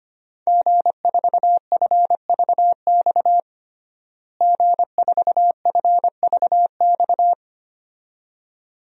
Morse code, heard as a keyed tone, about 25 wpm, 700 Hz. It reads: G4FVX G4FVX
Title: DXCC-100 Call Signs - 5 Characters - Repeated 2x - 25wpm